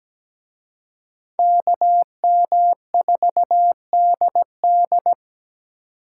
Morse code, heard as a keyed tone, about 20 wpm, 700 Hz. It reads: KM4DD
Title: Speed Racing - USA Call Signs 17wpm